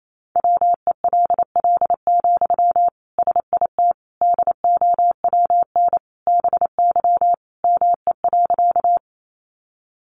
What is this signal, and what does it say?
Morse code, heard as a keyed tone, about 28 wpm, 700 Hz. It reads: WELL, SIT DOWN BY ME.
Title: Sentences from Top 500 Words 28wpm